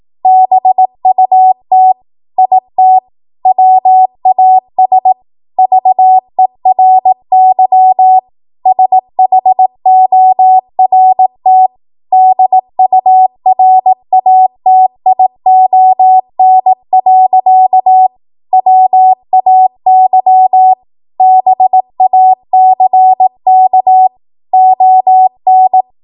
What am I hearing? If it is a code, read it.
Morse code, heard as a keyed tone, about 18 wpm, 750 Hz. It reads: BUT IT WAS VERY SHORT DURATION. WAY BACK ON
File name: Archive / 18 WPM / 18WPM